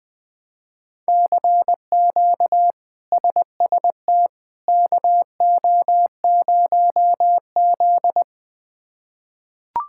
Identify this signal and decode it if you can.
Morse code, heard as a keyed tone, about 20 wpm, 700 Hz. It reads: CQ SST KO0Z
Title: K1USN SST QSO Practice - 20wpm